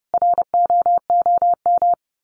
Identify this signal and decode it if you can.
Morse code, heard as a keyed tone, about 30 wpm, 700 Hz. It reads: ROOM
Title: Sets of 5 Words Encore 30wpm